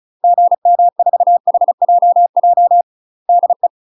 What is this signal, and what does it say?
Morse code, heard as a keyed tone, about 36 wpm, 700 Hz. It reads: GM4HJJ DE
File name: UK MASTER QSOs TRAINING / 35WPM / UK MASTER QSOs 0043